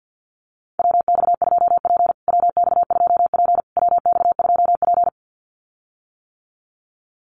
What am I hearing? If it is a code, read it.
Morse code, heard as a keyed tone, about 50 wpm, 700 Hz. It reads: WX2P WX2P WX2P
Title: USA Call Signs Repeated 3x 50wpm